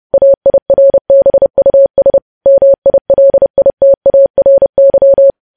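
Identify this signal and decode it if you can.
Morse code, heard as a keyed tone, about 30 wpm, 550 Hz. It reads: AIRBUS MILITARY
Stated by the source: News headlines sent in Morse at 30 words per minute.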